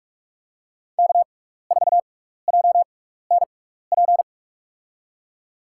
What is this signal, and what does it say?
Morse code, heard as a keyed tone, about 43 wpm, spaced as for 15 wpm, 700 Hz. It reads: K4JNP
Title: USA Call Signs - 6x Character Spacing - 45wpm